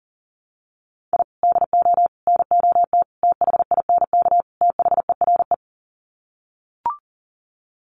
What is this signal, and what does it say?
Morse code, heard as a keyed tone, about 40 wpm, 700 Hz. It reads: I DO NOT THINK THERE
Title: Sets of 5 Words 40wpm